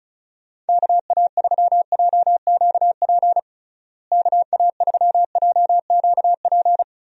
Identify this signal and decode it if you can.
Morse code, heard as a keyed tone, about 34 wpm, 700 Hz. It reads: KA3JQP KA3JQP
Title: USA Call Signs - 6 Characters - Repeated 3x - 35wpm